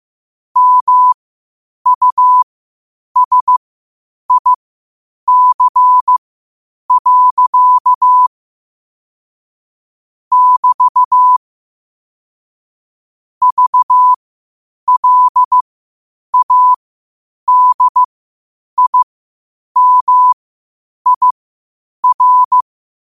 Morse code, heard as a keyed tone, about 15 wpm, spaced as for 8 wpm, 1000 Hz. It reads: MUSIC. <BT> VLADIMIR
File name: qotdfiles / 8wpm